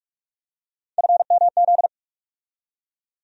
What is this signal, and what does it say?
Morse code, heard as a keyed tone, about 47 wpm, 700 Hz. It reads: FMZ
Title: USA Call Sign Postfixes - 3 Letters 45wpm